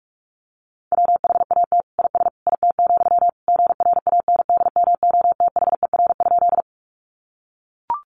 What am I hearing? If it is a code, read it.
Morse code, heard as a keyed tone, about 44 wpm, 700 Hz. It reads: WHAT IS IT, GRANDMOTHER?
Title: Sentences from Top 1200 Words 45wpm